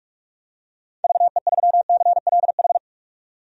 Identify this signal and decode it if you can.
Morse code, heard as a keyed tone, about 46 wpm, 700 Hz. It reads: VE3CLH
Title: DXCC-100 Call Signs - 6 Characters - 45wpm